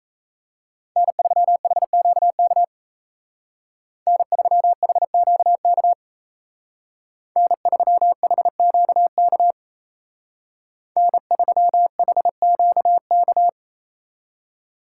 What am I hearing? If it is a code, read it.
Morse code, heard as a keyed tone, about 37 wpm, 700 Hz. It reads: N3HQK N3HQK N3HQK N3HQK